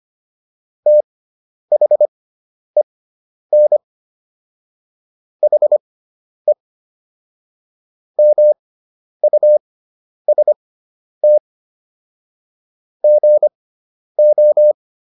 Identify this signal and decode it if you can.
Morse code, heard as a keyed tone, about 25 wpm, spaced as for 10 wpm, 600 Hz. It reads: THEN HE MUST GO